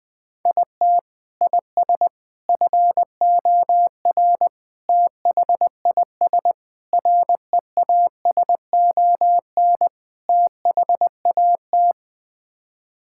Morse code, heard as a keyed tone, about 20 wpm, 700 Hz. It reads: IT IS FOR THIS REASON THAT
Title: Sets of 6 Words - 20wpm